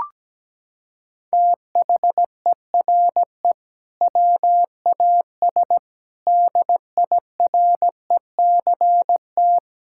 Morse code, heard as a keyed tone, about 17 wpm, 700 Hz. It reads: THERE WAS DIRECT
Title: Sentences from Top 1000 Words - Review Part 6 - 17wpm